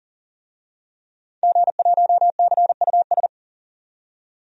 Morse code, heard as a keyed tone, about 40 wpm, 700 Hz. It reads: G1CUS
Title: DXCC-100 Call Signs - 5 Characters - 40wpm